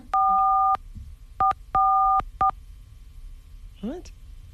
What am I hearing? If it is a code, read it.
DTMF, heard as telephone touch tones, tones of about 310 ms, with speech. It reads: 4 444